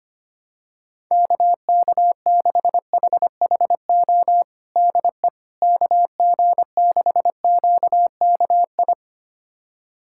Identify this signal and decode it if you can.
Morse code, heard as a keyed tone, about 25 wpm, 700 Hz. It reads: KK6HHO DE KG6QKI
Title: USA Callsign DE Callsign 25wpm